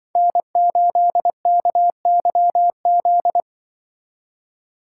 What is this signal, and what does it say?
Morse code, heard as a keyed tone, about 24 wpm, 700 Hz. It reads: N8KYZ
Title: Speed Racing - USA 1x3 Call Signs 20wpm